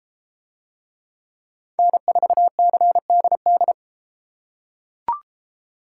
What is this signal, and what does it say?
Morse code, heard as a keyed tone, about 29 wpm, 700 Hz. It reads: N4CDD